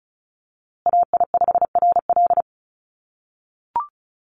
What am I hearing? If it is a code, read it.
Morse code, heard as a keyed tone, about 34 wpm, 700 Hz. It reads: AI5RL
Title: USA Call Signs Repeated 2x 35wpm